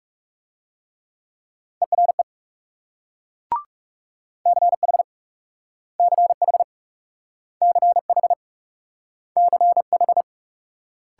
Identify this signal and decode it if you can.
Morse code, heard as a keyed tone, about 38 wpm, 700 Hz. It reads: ERE CH CH CH CH